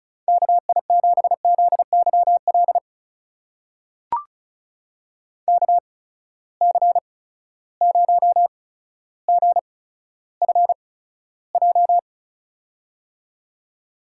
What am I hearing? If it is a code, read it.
Morse code, heard as a keyed tone, about 34 wpm, spaced as for 15 wpm, 700 Hz. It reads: KI7ZYL KC0GFJ
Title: USA Call Signs - 6 Characters - 8x Character Spacing - 35wpm